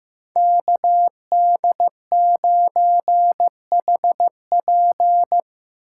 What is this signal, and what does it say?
Morse code, heard as a keyed tone, about 15 wpm, 700 Hz. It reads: KD9HP
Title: USA Call Signs - 5 Characters - 4x Character Spacing - 15wpm